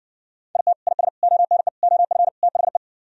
Morse code, heard as a keyed tone, about 60 wpm, 700 Hz. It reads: IT IS ONE OF THE